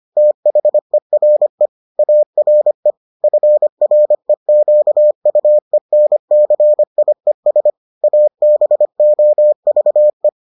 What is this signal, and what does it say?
Morse code, heard as a keyed tone, about 25 wpm, 600 Hz. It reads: THERE ARE FREQUENCIES ABOVE